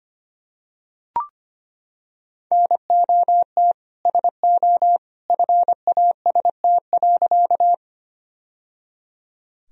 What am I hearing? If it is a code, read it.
Morse code, heard as a keyed tone, about 25 wpm, 700 Hz. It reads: NOT SO FAST.
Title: Sentences from Top 1000 Words - Review Part 2 - 25wpm